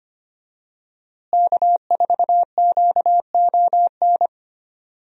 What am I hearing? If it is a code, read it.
Morse code, heard as a keyed tone, about 25 wpm, 700 Hz. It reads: K4QON
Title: USA Call Signs - 5 Characters - Repeated 2x - 25wpm